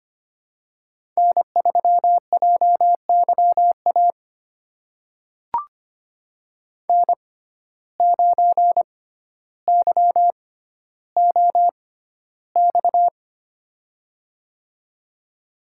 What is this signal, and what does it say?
Morse code, heard as a keyed tone, about 25 wpm, spaced as for 13 wpm, 700 Hz. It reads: N3JYA N9YOX
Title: USA Call Signs - 5 Characters - 6x Character Spacing - 25wpm